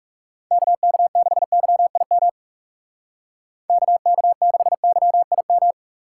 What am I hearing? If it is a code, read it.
Morse code, heard as a keyed tone, about 36 wpm, 700 Hz. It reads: KK6YIM KK6YIM
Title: Speed Racing - USA 2x3 Call Signs 30wpm